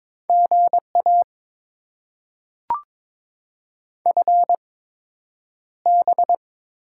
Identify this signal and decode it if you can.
Morse code, heard as a keyed tone, about 22 wpm, spaced as for 9 wpm, 700 Hz. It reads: GA FB
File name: Top 20 QSO Elements - 8x Character Spacing - 22wpm